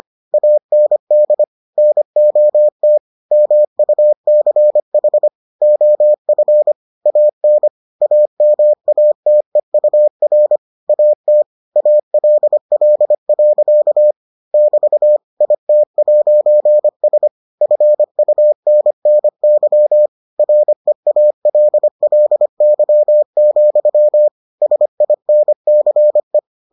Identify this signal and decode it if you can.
Morse code, heard as a keyed tone, about 25 wpm, 600 Hz. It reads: AND NOT MUCH OF AN AMATEUR AT ALL. <BT> IT'S FUNNY REALLY, SINCE